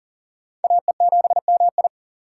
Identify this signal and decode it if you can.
Morse code, heard as a keyed tone, about 40 wpm, 700 Hz. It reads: AE7MI
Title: USA Call Signs - 5 Characters - 2x Character Spacing - 40wpm